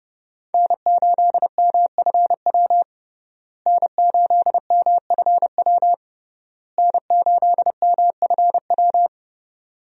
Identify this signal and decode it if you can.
Morse code, heard as a keyed tone, about 30 wpm, 700 Hz. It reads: N8MFW N8MFW N8MFW
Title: USA Call Signs - 5 Characters - Repeated 3x - 30wpm